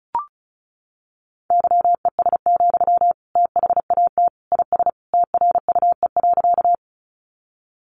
Morse code, heard as a keyed tone, about 34 wpm, 700 Hz. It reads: YES, THAT IS TRUE.
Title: Sentences from Top 500 Words 35wpm